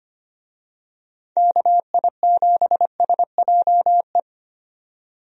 Morse code, heard as a keyed tone, about 25 wpm, 700 Hz. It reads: KI7SJE
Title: USA Call Signs - 6 Characters - 25wpm